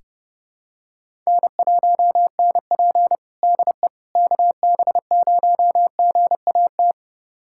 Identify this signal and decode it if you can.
Morse code, heard as a keyed tone, about 32 wpm, 700 Hz. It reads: N1NP DE KB0GAT
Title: USA Callsign DE Callsign 30wpm